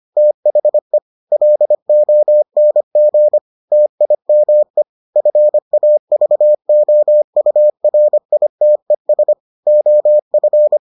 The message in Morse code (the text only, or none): THE LONG TIME FAVOURITES OF